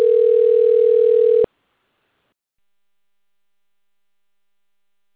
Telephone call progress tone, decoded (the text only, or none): ringback tone